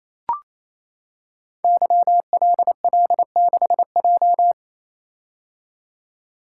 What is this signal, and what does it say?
Morse code, heard as a keyed tone, about 29 wpm, 700 Hz. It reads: YLL6J
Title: Code Groups - 5 Letter:Numbers - 28wpm